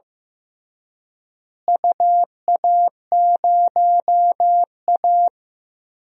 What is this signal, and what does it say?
Morse code, heard as a keyed tone, about 16 wpm, 700 Hz. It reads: UA0A
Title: DXCC-100 Call Signs - 4 Characters - Repeated 3x - 15wpm